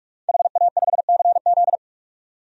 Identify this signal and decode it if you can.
Morse code, heard as a keyed tone, about 41 wpm, 700 Hz. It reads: HA5CZ
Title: DXCC-100 Call Signs - 5 Characters - 45wpm